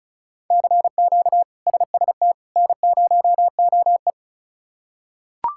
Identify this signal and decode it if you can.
Morse code, heard as a keyed tone, about 34 wpm, 700 Hz. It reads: CQ SST N0OE
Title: K1USN SST QSO Practice - 35wpm